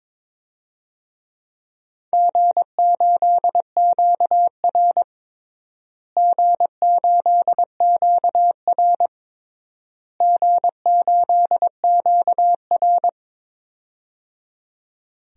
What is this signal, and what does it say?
Morse code, heard as a keyed tone, about 22 wpm, 700 Hz. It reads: G8QR G8QR G8QR
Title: DXCC-100 Call Signs - 4 Characters - Repeated 3x - 22wpm